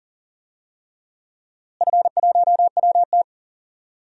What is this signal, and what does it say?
Morse code, heard as a keyed tone, about 40 wpm, 700 Hz. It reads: F1WT
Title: DXCC-100 Call Signs - 4 Characters - Repeated 3x - 40wpm